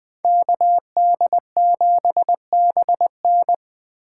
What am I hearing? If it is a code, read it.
Morse code, heard as a keyed tone, about 21 wpm, 700 Hz. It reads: KD7BN